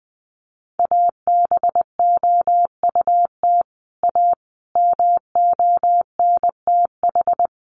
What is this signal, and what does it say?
Morse code, heard as a keyed tone, about 20 wpm, 700 Hz. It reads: ABOUT A MONTH
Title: Sentences from Top 800 Words 20wpm